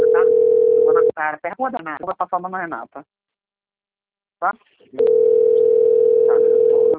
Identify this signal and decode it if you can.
Ringback tone (North America), 440 + 480 Hz, with speech.
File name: rg-1600-unknown- - - .43201